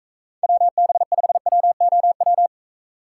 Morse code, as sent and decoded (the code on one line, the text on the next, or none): .-- -... ..... .-- --- .--
WB5WOW